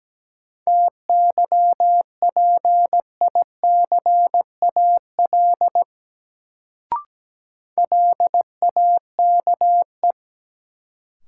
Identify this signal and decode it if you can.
Morse code, heard as a keyed tone, about 17 wpm, 700 Hz. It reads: TYPICAL LAKE
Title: Sentences from Top 1100 Words - New Word Drill Alt - 17wpm